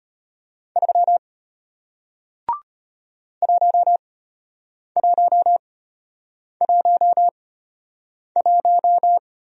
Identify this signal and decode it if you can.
Morse code, heard as a keyed tone, about 33 wpm, spaced as for 8 wpm, 700 Hz. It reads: 3 1111